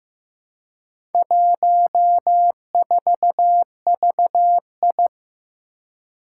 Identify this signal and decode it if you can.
Morse code, heard as a keyed tone, about 15 wpm, 700 Hz. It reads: 14VI
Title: Four Challenging Letter-Number Combinations 15wpm